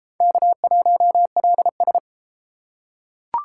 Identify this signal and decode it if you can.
Morse code, heard as a keyed tone, about 33 wpm, 700 Hz. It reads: K1LS